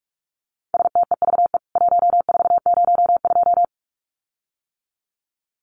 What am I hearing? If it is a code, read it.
Morse code, heard as a keyed tone, about 44 wpm, 700 Hz. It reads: STEVE 1402
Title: CWT Exchange 45wpm